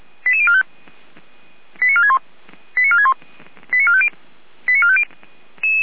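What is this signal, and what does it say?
Five-tone selective call (ZVEI), heard as quick runs of five tones, tones of about 70 ms. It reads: 88045 89461 89461 89450 89450
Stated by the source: ZVEI five-tone call received on an SDR.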